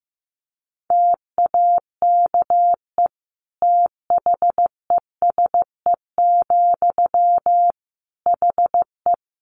Morse code, heard as a keyed tone, about 15 wpm, 700 Hz. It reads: TAKE THESE, HE
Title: Sentences from Top 300 Words 15wpm